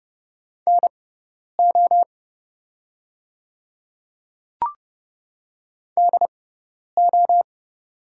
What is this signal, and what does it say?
Morse code, heard as a keyed tone, about 30 wpm, spaced as for 10 wpm, 700 Hz. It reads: NO DO
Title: 6x Character Spacing - Top 300 Words - 30wpm